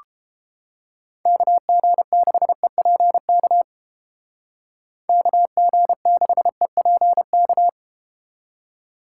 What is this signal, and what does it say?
Morse code, heard as a keyed tone, about 31 wpm, 700 Hz. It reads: KG6EPK KG6EPK